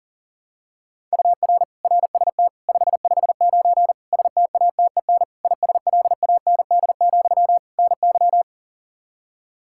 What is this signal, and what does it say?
Morse code, heard as a keyed tone, about 40 wpm, 700 Hz. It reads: UR RST 559 STATEN ISLAND, NY
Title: Practice QSO - Exchange 1 - Beginning - 40wpm